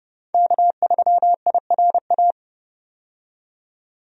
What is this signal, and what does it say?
Morse code, heard as a keyed tone, about 30 wpm, 700 Hz. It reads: K3IRA
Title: USA Call Signs - 5 Characters - 30wpm